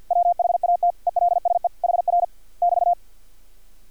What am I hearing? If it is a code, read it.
Morse code, heard as a keyed tone, about 50 wpm, 700 Hz. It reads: WHAT ELSE HR <BT>
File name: archived / CRSnet